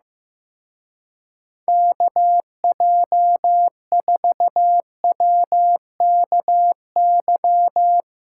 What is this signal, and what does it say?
Morse code, heard as a keyed tone, about 15 wpm, 700 Hz. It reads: KJ4WKY
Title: Speed Racing - USA 2x3 Call Signs 15wpm